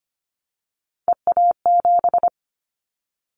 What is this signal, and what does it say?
Morse code, heard as a keyed tone, about 25 wpm, 700 Hz. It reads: EA7
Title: Callsign Prefixes for Top 25 DXCC - 25wpm